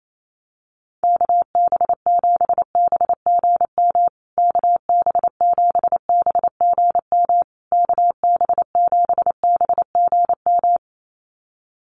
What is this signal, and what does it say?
Morse code, heard as a keyed tone, about 28 wpm, 700 Hz. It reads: KB7BGM KB7BGM KB7BGM
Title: USA Call Signs Repeated 3x 28wpm